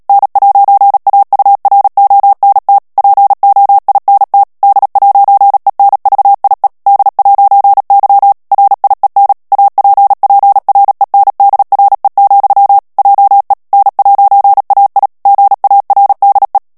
Morse code, heard as a keyed tone, about 37 wpm, 800 Hz. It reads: N'AURONT POINT D'ENVIE D'Y RIEN APPRENDRE, JE N'AI GARDE
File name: CW / FICHIERS-MP3 / FRANCAIS TEXTE CLAIR 2 VITESSE 35 MOTS MINUTES